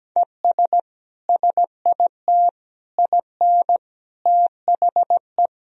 Morse code, heard as a keyed tone, about 17 wpm, 700 Hz. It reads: ES SIT IN THE
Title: Sentences from Top 500 Words and Abbreviations 17wpm